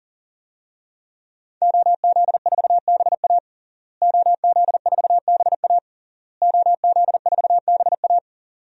Morse code, heard as a keyed tone, about 38 wpm, 700 Hz. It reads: OZ4BA OZ4BA OZ4BA
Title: DXCC-100 Call Signs - 5 Characters - Repeated 3x - 40wpm